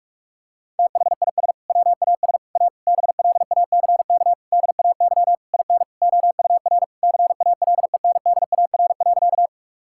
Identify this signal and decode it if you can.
Morse code, heard as a keyed tone, about 45 wpm, 700 Hz. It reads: THIS WAS A BLACK DAY IN OUR CALENDAR.
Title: Sentences from Top 1600 Words 45wpm